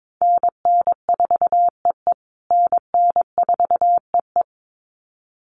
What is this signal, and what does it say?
Morse code, heard as a keyed tone, about 22 wpm, 700 Hz. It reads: NN4EE NN4EE
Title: USA Call Signs Repeated 2x 22wpm